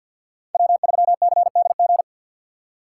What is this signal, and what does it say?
Morse code, heard as a keyed tone, about 53 wpm, 690 Hz. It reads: W3CDG
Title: USA Call Signs - 2x Character Spacing - 50wpm